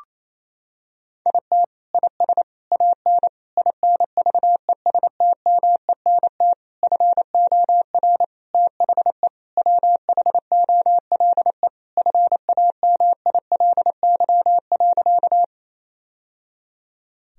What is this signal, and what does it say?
Morse code, heard as a keyed tone, about 28 wpm, 700 Hz. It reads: IT IS AN INVESTMENT FOR THE WHOLE FAMILY.
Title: Sentences from Top 1000 Words - Review Part 4 - 28wpm